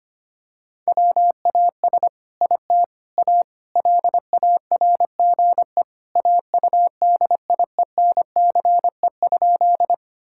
Morse code, heard as a keyed tone, about 25 wpm, 700 Hz. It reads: WAS IT A LARGE AUDIENCE?
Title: Sentences from Top 1000 Words - Review Part 5 - 25wpm